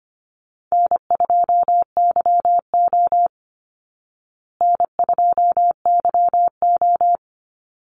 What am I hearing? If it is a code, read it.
Morse code, heard as a keyed tone, about 25 wpm, 700 Hz. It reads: N2YO N2YO
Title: Easy Call Signs Repeated 2x - 25wpm